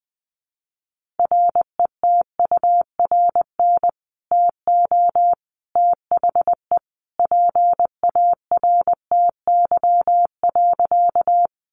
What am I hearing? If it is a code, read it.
Morse code, heard as a keyed tone, about 19 wpm, 700 Hz. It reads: RETURN TO THE PARTY.
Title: Sentences from Top 1400 Words 20wpm